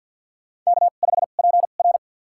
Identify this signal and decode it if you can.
Morse code, heard as a keyed tone, about 52 wpm, spaced as for 36 wpm, 700 Hz. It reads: K5PR
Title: USA Call Signs - 2x Character Spacing - 50wpm